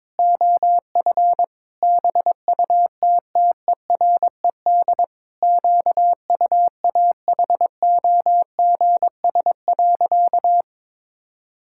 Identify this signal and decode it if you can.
Morse code, heard as a keyed tone, about 22 wpm, 700 Hz. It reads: OF BUTTERED QUAHOGS.
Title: Pangram Challenge v2 - 22wpm